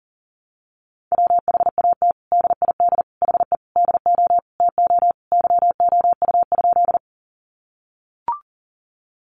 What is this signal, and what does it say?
Morse code, heard as a keyed tone, about 40 wpm, 700 Hz. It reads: WHAT DID HE DO TO YOU?
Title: Sentences from Top 100 Words 40wpm